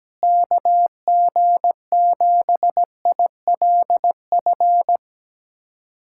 Morse code, heard as a keyed tone, about 17 wpm, 700 Hz. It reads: KG7ILF
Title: USA Callsign DE Callsign 17wpm